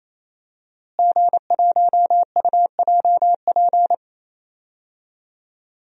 Morse code, heard as a keyed tone, about 27 wpm, 700 Hz. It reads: G1UJP
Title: DXCC-100 Call Signs - 5 Characters - Repeated 3x - 28wpm